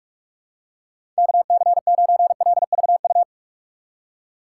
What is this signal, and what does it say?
Morse code, heard as a keyed tone, about 46 wpm, 700 Hz. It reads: KC9LVU
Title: USA Call Signs - 6 Characters - 6x Character Spacing - 45wpm